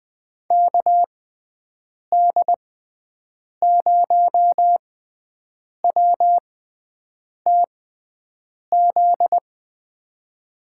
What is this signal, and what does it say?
Morse code, heard as a keyed tone, about 20 wpm, spaced as for 7 wpm, 700 Hz. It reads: KD0WTZ
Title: USA Call Signs - 6 Characters - 6x Character Spacing - 20wpm